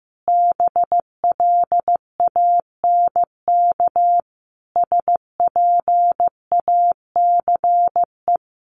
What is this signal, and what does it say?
Morse code, heard as a keyed tone, about 14 wpm, 700 Hz. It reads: BLANK SPACE
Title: Sentences from Top 1500 Words 15wpm